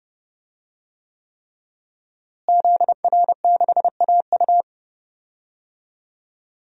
Morse code, heard as a keyed tone, about 32 wpm, 700 Hz. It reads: ZR6AU
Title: DXCC-100 Call Signs - 5 Characters - Repeated 3x - 30wpm